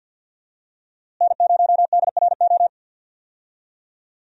Morse code, heard as a keyed tone, about 53 wpm, 700 Hz. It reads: N0DRO
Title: USA Call Signs - 5 Characters - Repeated 3x - 50wpm